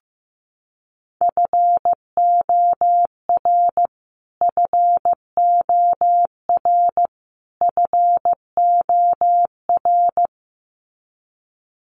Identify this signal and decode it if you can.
Morse code, heard as a keyed tone, about 15 wpm, 700 Hz. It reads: FOR FOR FOR
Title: Top 500 Words Repeated 15wpm